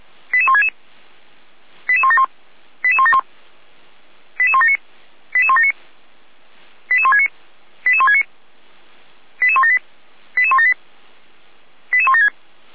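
Five-tone selective call (ZVEI), heard as quick runs of five tones, tones of about 80 ms. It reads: 88166 88171 88171 88179 88179 88169 88169 88178 88178 88176
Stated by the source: ZVEI five-tone selective call received with an SDR.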